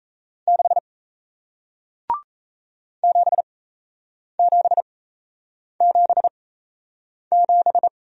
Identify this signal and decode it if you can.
Morse code, heard as a keyed tone, about 38 wpm, spaced as for 8 wpm, 700 Hz. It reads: 6 7777